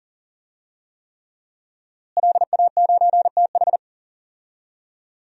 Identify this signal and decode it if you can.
Morse code, heard as a keyed tone, about 40 wpm, 700 Hz. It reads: LA9TH
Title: DXCC-100 Call Signs - 5 Characters - Repeated 3x - 40wpm